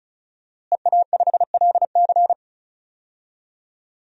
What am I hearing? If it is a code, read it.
Morse code, heard as a keyed tone, about 36 wpm, 700 Hz. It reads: EA5LC
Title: DXCC-100 Call Signs - 5 Characters - Repeated 2x - 35wpm